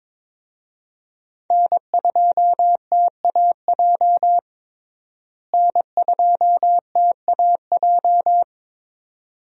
Morse code, heard as a keyed tone, about 23 wpm, 700 Hz. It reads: N2TAJ N2TAJ
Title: USA Call Signs - 5 Characters - Repeated 2x - 22wpm